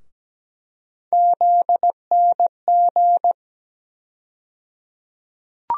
Morse code, heard as a keyed tone, about 17 wpm, 700 Hz. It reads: ZNG
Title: USA Call Sign Postfixes - 3 Letters 17wpm